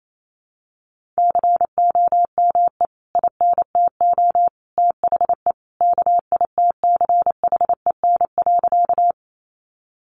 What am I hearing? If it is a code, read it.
Morse code, heard as a keyed tone, about 28 wpm, 700 Hz. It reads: COME INTO THE KITCHEN.
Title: Sentences from Top 1400 Words 28wpm